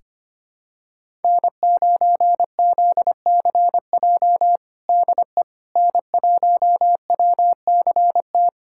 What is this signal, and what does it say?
Morse code, heard as a keyed tone, about 25 wpm, 700 Hz. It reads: N9ZCJ DE N1WCT
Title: USA Callsign DE Callsign - 5 Characters - 25wpm